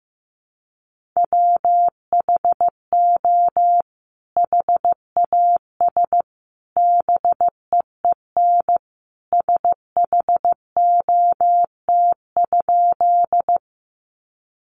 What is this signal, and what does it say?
Morse code, heard as a keyed tone, about 15 wpm, 700 Hz. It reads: WHO HAS BEEN SHOT?
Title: Sentences from Top 1100 Words 15wpm